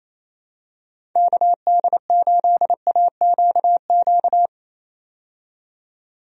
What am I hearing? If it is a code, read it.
Morse code, heard as a keyed tone, about 28 wpm, 700 Hz. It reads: KD8AQQ
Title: USA Call Signs - 6 Characters - 28wpm